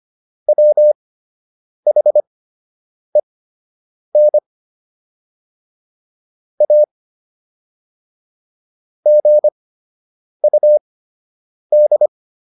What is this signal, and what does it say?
Morse code, heard as a keyed tone, about 25 wpm, spaced as for 8 wpm, 600 Hz. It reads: WHEN A GUD